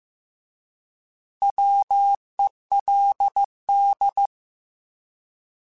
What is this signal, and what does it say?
Morse code, heard as a keyed tone, about 15 wpm, 800 Hz. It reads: WELD